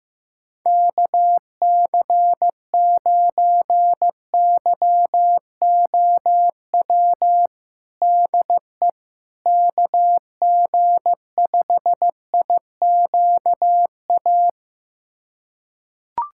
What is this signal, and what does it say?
Morse code, heard as a keyed tone, about 15 wpm, 700 Hz. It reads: KC9YOW DE KG5IQA
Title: USA Callsign DE Callsign - 6 Characters - 15wpm